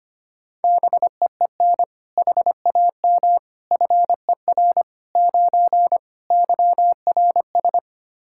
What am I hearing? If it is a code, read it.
Morse code, heard as a keyed tone, about 25 wpm, 700 Hz. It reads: BEEN HAM FER 9 YRS
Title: Practice QSO - Exchange 3 - More Info - 25wpm